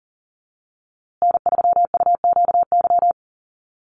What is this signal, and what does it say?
Morse code, heard as a keyed tone, about 39 wpm, 700 Hz. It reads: N3UQY
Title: USA Call Signs Repeated 3x 40wpm